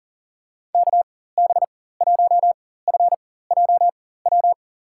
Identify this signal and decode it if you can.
Morse code, heard as a keyed tone, about 37 wpm, spaced as for 18 wpm, 700 Hz. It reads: KB1FJW
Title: USA Call Signs - 6 Characters - 4x Character Spacing - 40wpm